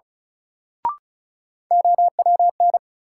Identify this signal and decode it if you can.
Morse code, heard as a keyed tone, about 36 wpm, 700 Hz. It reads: OWN
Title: Sentences from Top 300 Words - New Word Drill - 35wpm